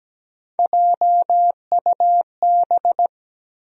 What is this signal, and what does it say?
Morse code, heard as a keyed tone, about 17 wpm, 700 Hz. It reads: JUB
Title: USA Call Sign Postfixes - 3 Letters 17wpm